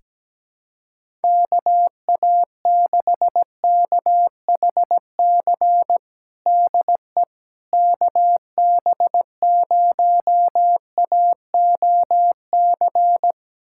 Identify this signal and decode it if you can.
Morse code, heard as a keyed tone, about 17 wpm, 700 Hz. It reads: KA6KHC DE KB0AOC
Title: USA Callsign DE Callsign - 6 Characters - 17wpm